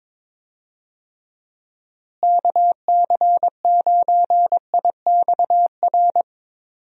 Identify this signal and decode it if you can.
Morse code, heard as a keyed tone, about 22 wpm, 700 Hz. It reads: KC9IXR
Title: USA Call Signs - 6 Characters - Repeated 3x - 22wpm